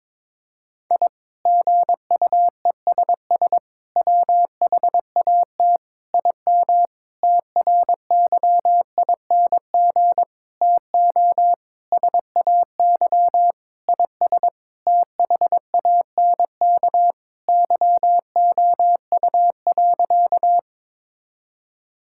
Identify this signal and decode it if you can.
Morse code, heard as a keyed tone, about 22 wpm, 700 Hz. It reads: I GUESS WHAT IM TRYING TO SAY IS THANK YOU.
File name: practice / 22wpm / Sentences with 8-Word Phrases - 22wpm